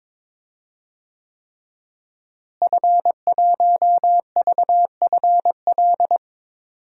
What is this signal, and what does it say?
Morse code, heard as a keyed tone, about 21 wpm, 700 Hz. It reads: F1VFL
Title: DXCC-100 Call Signs - 5 Characters - Repeated 3x - 22wpm